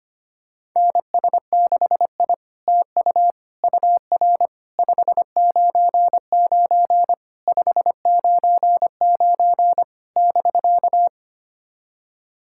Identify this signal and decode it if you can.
Morse code, heard as a keyed tone, about 25 wpm, 700 Hz. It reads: NS6I TU UR 599 599 <BK>
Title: SOTA QSO Practice 25wpm